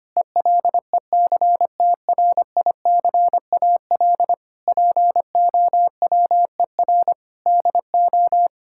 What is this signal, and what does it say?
Morse code, heard as a keyed tone, about 25 wpm, 700 Hz. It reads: ELECTRICAL POWER DO
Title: Sentences from Top 1000 Words - Review Part 6 - 25wpm